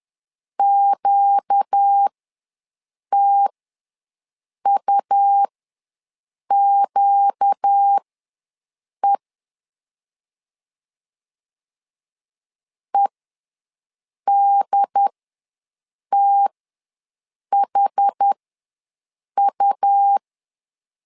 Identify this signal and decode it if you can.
Morse code, heard as a keyed tone, about 10 wpm, spaced as for 6 wpm, 800 Hz. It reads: QTUQE EDTHU
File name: LEZIONI / 4 Q ws 10wpm 2306